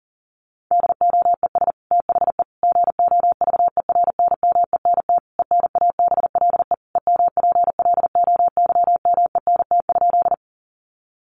Morse code, heard as a keyed tone, about 40 wpm, 700 Hz. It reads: DOES THE GOVERNMENT ENABLE EMPLOYMENT?
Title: Sentences from Top 1500 Words 40wpm